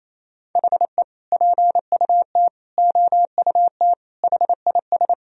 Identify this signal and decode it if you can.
Morse code, heard as a keyed tone, about 28 wpm, 700 Hz. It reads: HE PUT OUT HIS
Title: Sentences from Top 1000 Words - Review Part 3 - 28wpm